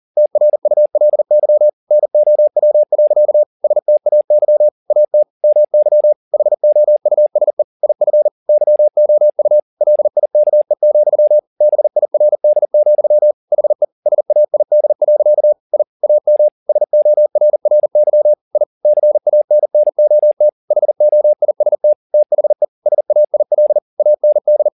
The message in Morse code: TRULY NOW. STAY AT MY HOUSE IF YOU LIKE, BIRD, HE SAID. I AM SORRY I CANNOT HOIST THE SAIL AND

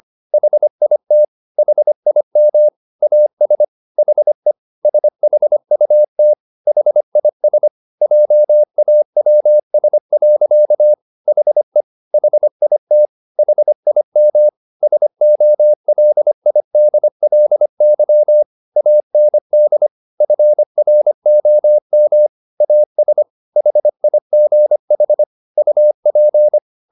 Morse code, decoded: HIT HIM AS HE SHUT HIS JAWS. HE HIT HIM SOLIDLY AND FROM AS HIGH UP

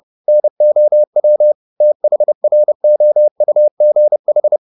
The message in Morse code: NOW THROUGH

